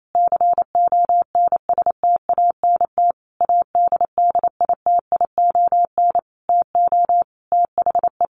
Morse code, heard as a keyed tone, about 28 wpm, 700 Hz. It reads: CONSTANT ADDITION TO THE
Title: Sentences from Top 1500 Words 28wpm